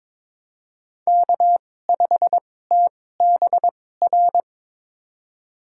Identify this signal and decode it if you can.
Morse code, heard as a keyed tone, about 22 wpm, spaced as for 16 wpm, 700 Hz. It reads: K5TBR
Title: USA Call Signs - 5 Characters - 2x Character Spacing - 22wpm